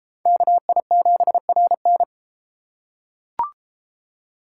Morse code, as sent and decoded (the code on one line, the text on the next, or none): -.- .. --... .-. -.
KI7RN